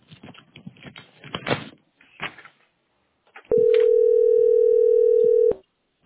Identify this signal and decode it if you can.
Ringback tone (North America), 440 + 480 Hz.